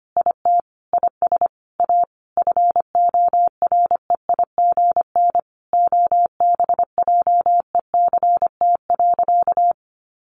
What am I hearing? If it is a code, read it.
Morse code, heard as a keyed tone, about 25 wpm, 700 Hz. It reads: IT IS A FOREIGN OBJECT.
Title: Sentences from Top 1400 Words 25wpm